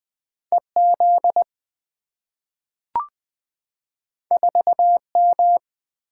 Morse code, heard as a keyed tone, about 20 wpm, 700 Hz. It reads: EZ 4M
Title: Two Letter-Number Combinations - 20wpm